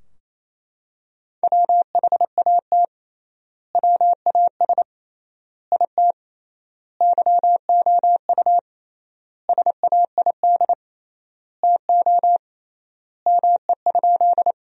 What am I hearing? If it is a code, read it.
Morse code, heard as a keyed tone, about 28 wpm, 700 Hz. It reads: WHAT WAS IT YOU SAID TO ME?
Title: 3x Word-Spacing - Sentences from Top 200 Words - 28wpm